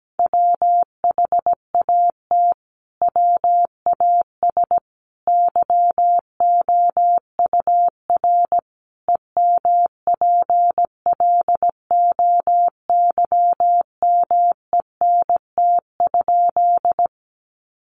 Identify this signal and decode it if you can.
Morse code, heard as a keyed tone, about 17 wpm, 700 Hz. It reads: WHAT WAS YOUR EMPLOYMENT?
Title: Sentences from Top 1500 Words 17wpm